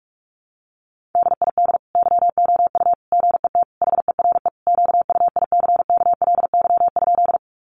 Morse code, heard as a keyed tone, about 44 wpm, 700 Hz. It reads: DID YOU GET HERE QUICKLY?
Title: Sentences from Top 700 Words 45wpm